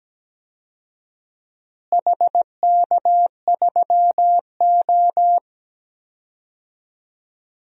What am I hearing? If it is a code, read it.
Morse code, heard as a keyed tone, about 17 wpm, 700 Hz. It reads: HK3O
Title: DXCC-100 Call Signs - 4 Characters - Repeated 3x - 17wpm